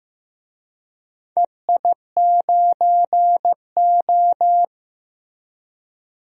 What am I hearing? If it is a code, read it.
Morse code, heard as a keyed tone, about 15 wpm, 700 Hz. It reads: EI9O
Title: DXCC-100 Call Signs - 4 Characters - Repeated 3x - 15wpm